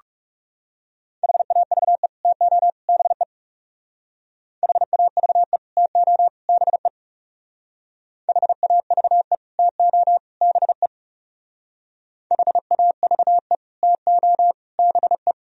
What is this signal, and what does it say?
Morse code, heard as a keyed tone, about 35 wpm, 700 Hz. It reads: HAVE TO BE HAVE TO BE HAVE TO BE HAVE TO BE
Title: Speed-Racing - Sets of 3 Words 30wpm